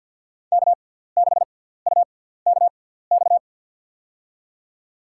Morse code, heard as a keyed tone, about 47 wpm, spaced as for 17 wpm, 690 Hz. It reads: K6UKX
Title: USA Call Signs - 5 Characters - 6x Character Spacing - 50wpm